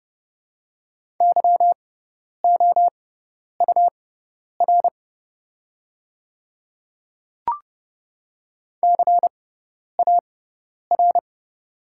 Morse code, heard as a keyed tone, about 32 wpm, spaced as for 10 wpm, 700 Hz. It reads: YOUR CAR
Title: 6x Character Spacing - Top 300 Words - 30wpm